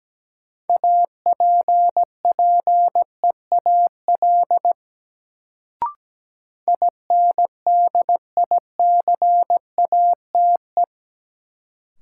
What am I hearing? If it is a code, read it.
Morse code, heard as a keyed tone, about 17 wpm, 700 Hz. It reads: APPEAL INDICATE
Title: Sentences from Top 1300 Words - New Word Drill - 17wpm